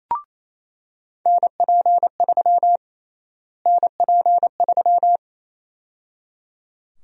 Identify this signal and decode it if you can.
Morse code, heard as a keyed tone, about 28 wpm, 700 Hz. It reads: NP3 NP3
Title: USA Call Sign Prefixes - Repeated 2x - 28wpm